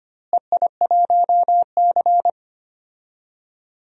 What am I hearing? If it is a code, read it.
Morse code, heard as a keyed tone, about 25 wpm, 700 Hz. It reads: EI1C
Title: DXCC-100 Call Signs - 4 Characters - Repeated 3x - 25wpm